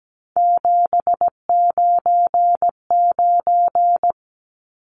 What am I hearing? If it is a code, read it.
Morse code, heard as a keyed tone, about 17 wpm, 700 Hz. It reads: 799
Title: Numbers - 3-Digit 17wpm